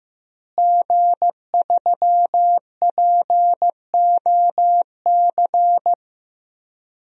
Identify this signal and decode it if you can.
Morse code, heard as a keyed tone, about 15 wpm, 700 Hz. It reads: G3POC
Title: DXCC-100 Call Signs - 5 Characters - Repeated 3x - 15wpm